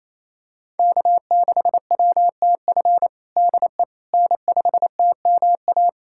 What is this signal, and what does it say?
Morse code, heard as a keyed tone, about 28 wpm, 700 Hz. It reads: K6WTF DE N5TMA
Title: USA Callsign DE Callsign - 5 Characters - 28wpm